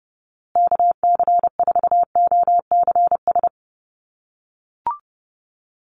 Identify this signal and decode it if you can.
Morse code, heard as a keyed tone, about 30 wpm, 700 Hz. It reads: KC4OCS
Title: USA Call Signs Repeated 2x 30wpm